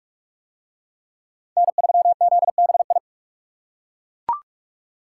Morse code, heard as a keyed tone, about 39 wpm, 700 Hz. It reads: N3ZBI